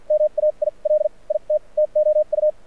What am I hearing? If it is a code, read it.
Morse code, heard as a keyed tone, about 54 wpm, spaced as for 41 wpm, 610 Hz, with radio noise. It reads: MAIL IT TOU
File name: archived / CRSnet